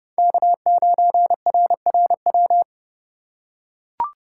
Morse code, heard as a keyed tone, about 32 wpm, 700 Hz. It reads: K9RRW
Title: USA Call Signs - 5 Characters - 3x Character Spacing - 30wpm